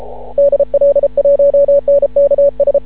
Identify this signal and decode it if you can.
Morse code, heard as a keyed tone, about 31 wpm, 560 Hz. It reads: DL1NKS
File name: CW / ha6oi 2